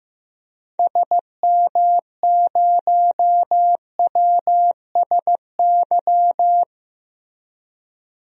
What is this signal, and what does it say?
Morse code, heard as a keyed tone, about 15 wpm, 700 Hz. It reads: SM0WSY
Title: DXCC-100 Call Signs - 6 Characters - 15wpm